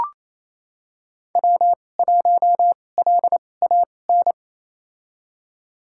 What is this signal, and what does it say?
Morse code, heard as a keyed tone, about 28 wpm, spaced as for 20 wpm, 700 Hz. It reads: W1LAN
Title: USA Call Signs - 5 Characters - 2x Character Spacing - 28wpm